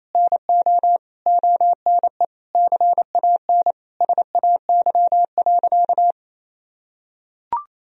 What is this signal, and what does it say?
Morse code, heard as a keyed tone, about 28 wpm, 700 Hz. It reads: NO ONE CAN SAY.
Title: Sentences from Top 1000 Words - Review Part 2 - 28wpm